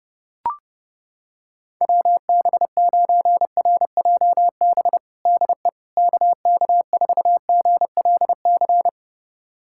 Morse code, heard as a keyed tone, about 30 wpm, 700 Hz. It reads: WB9RJB DE KK4GLC
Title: USA Callsign DE Callsign 30wpm